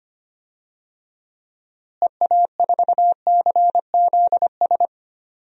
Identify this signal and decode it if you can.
Morse code, heard as a keyed tone, about 25 wpm, 700 Hz. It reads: EA4CZS
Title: DXCC-100 Call Signs - 6 Characters - Repeated 3x - 25wpm